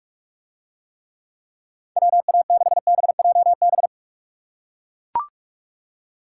Morse code, heard as a keyed tone, about 43 wpm, 700 Hz. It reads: WA6BJB